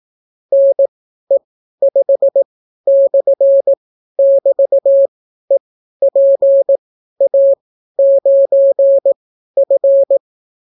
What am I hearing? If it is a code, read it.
Morse code, heard as a keyed tone, about 18 wpm, spaced as for 12 wpm, 550 Hz. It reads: NE5/<BT>EPA9F